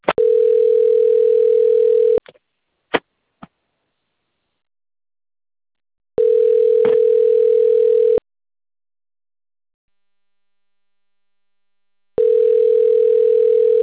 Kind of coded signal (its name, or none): telephone call progress tone